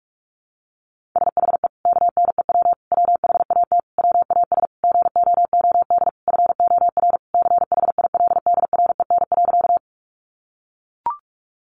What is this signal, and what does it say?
Morse code, heard as a keyed tone, about 44 wpm, 700 Hz. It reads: SHE KNEW WHAT WAS GOOD FOR CHILDREN.
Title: Sentences from Top 400 Words 45wpm